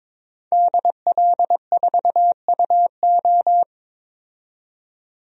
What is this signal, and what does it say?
Morse code, heard as a keyed tone, about 22 wpm, 700 Hz. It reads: DL4UO
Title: DXCC-100 Call Signs - 5 Characters - 22wpm